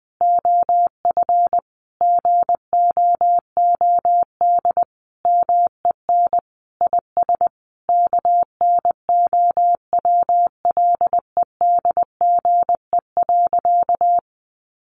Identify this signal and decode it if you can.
Morse code, heard as a keyed tone, about 20 wpm, 700 Hz. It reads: OF GOOD MEN IS KNOWLEDGE.
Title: Sentences from Top 1100 Words 20wpm